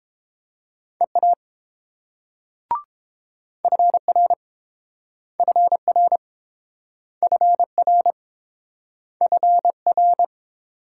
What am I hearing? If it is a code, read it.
Morse code, heard as a keyed tone, about 29 wpm, 700 Hz. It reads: EA FR FR FR FR